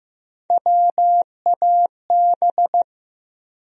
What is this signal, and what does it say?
Morse code, heard as a keyed tone, about 15 wpm, 700 Hz. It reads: WAB
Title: USA Call Sign Postfixes - 3 Letters 15wpm